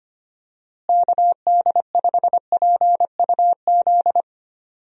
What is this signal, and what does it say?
Morse code, heard as a keyed tone, about 25 wpm, 700 Hz. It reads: KD5PUZ
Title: USA Call Signs - 6 Characters - Repeated 2x - 25wpm